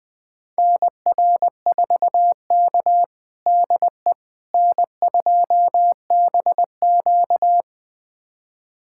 Morse code, heard as a keyed tone, about 21 wpm, 700 Hz. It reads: NR4K DE N2BQ
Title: USA Callsign DE Callsign - 4 Characters - 20wpm